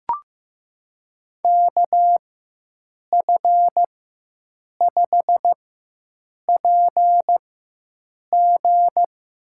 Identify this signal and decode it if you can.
Morse code, heard as a keyed tone, about 15 wpm, spaced as for 7 wpm, 700 Hz. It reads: KF5PG
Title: USA Call Signs - 6 Characters - 4x Character Spacing - 15wpm